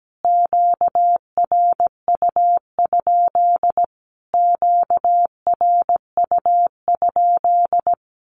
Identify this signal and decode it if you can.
Morse code, heard as a keyed tone, about 17 wpm, 700 Hz. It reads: QRU? QRU?
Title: Q-Codes Repeated 3x 17wpm